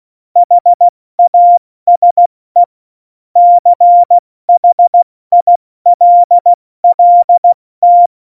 Morse code, heard as a keyed tone, about 16 wpm, 700 Hz. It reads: HASE CHILLT
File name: Morsekurs / Witze / Witz 08. Dezember 2020, 70 F80